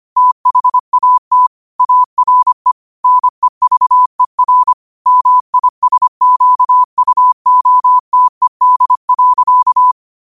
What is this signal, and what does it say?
Morse code, heard as a keyed tone, about 25 wpm, 1000 Hz. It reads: THAT ARE NEVER MISQUOTED.
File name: qotdfiles / 25wpm